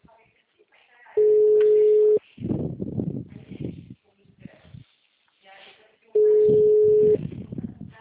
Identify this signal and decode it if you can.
Ringback tone (Europe, 425 Hz), with speech.
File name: out- -2031- - - .2658